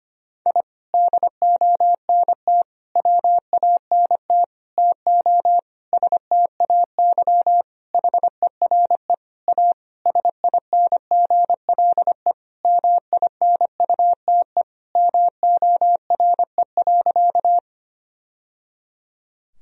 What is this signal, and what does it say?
Morse code, heard as a keyed tone, about 25 wpm, 700 Hz. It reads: I DONT WANT TO STAY HERE A SINGLE MINUTE MORE.
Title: Sentences from Top 1000 Words - Review Part 4 - 25wpm